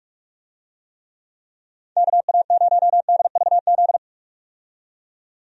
Morse code, heard as a keyed tone, about 46 wpm, 700 Hz. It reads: KA0DVZ
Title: USA Call Signs - 6 Characters - 45wpm